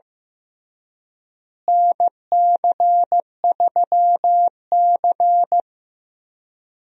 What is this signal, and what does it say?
Morse code, heard as a keyed tone, about 16 wpm, 700 Hz. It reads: NC3C
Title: USA Call Signs - 4 Characters - Repeated 3x - 15wpm